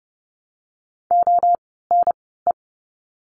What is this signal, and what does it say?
Morse code, heard as a keyed tone, about 30 wpm, 700 Hz. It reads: O N E
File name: Lesson 16f - Letter c - Words 30wpm